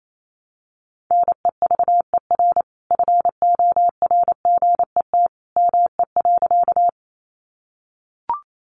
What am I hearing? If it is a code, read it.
Morse code, heard as a keyed tone, about 28 wpm, 700 Hz. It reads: NEVER FORGET ME.
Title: Sentences from Top 1200 Words 28wpm